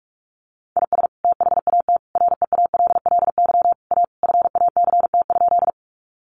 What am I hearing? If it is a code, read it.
Morse code, heard as a keyed tone, about 44 wpm, 700 Hz. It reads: IS THAT REALLY A FACT?
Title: Sentences from Top 600 Words 45wpm